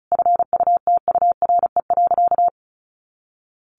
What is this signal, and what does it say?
Morse code, heard as a keyed tone, about 34 wpm, 700 Hz. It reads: FUTURE.
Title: A Sound Of Thunder 35wpm